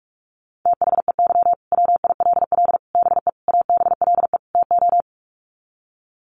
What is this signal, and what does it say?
Morse code, heard as a keyed tone, about 44 wpm, 700 Hz. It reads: THEY WILL BE ABLE TO
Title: Sets of 5 Words Encore 45wpm